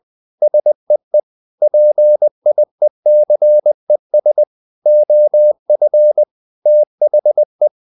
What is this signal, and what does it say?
Morse code, heard as a keyed tone, about 20 wpm, 600 Hz. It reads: SEE PIECES OF THE
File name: the old man and the sea part 2 20wpm